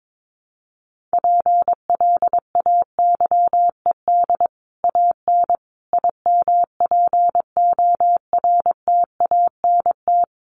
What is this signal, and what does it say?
Morse code, heard as a keyed tone, about 22 wpm, 700 Hz. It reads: PLAYED AN IMPORTANT
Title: Sets of 5 Words Encore2 22wpm